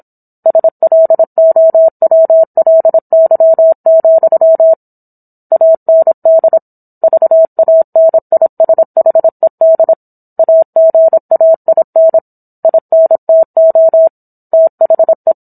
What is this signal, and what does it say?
Morse code, heard as a keyed tone, about 27 wpm, 650 Hz. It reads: SLOWLY, AND VANISHED AGAIN INTO THE